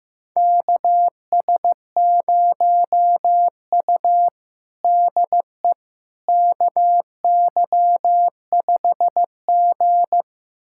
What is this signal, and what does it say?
Morse code, heard as a keyed tone, about 15 wpm, 700 Hz. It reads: KS0U DE KY5G
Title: USA Callsign DE Callsign - 4 Characters - 15wpm